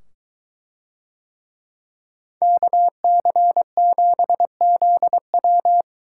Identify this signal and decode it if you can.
Morse code, heard as a keyed tone, about 23 wpm, 700 Hz. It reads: KC7ZW